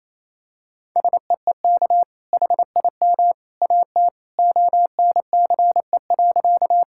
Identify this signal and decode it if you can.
Morse code, heard as a keyed tone, about 28 wpm, 700 Hz. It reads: SEEK HIM AT ONCE.
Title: Sentences from Top 1600 Words 28wpm